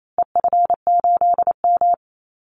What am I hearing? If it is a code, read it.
Morse code, heard as a keyed tone, about 28 wpm, 700 Hz. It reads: EF8M
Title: Call Signs Repeated 2x - 28wpm